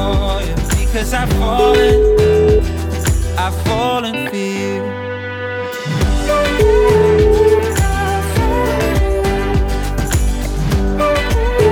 Ringback tone (Europe, 425 Hz), with noise.